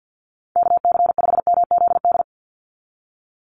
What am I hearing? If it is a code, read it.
Morse code, heard as a keyed tone, about 50 wpm, 700 Hz. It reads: KC5MZD
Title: USA Call Signs 50wpm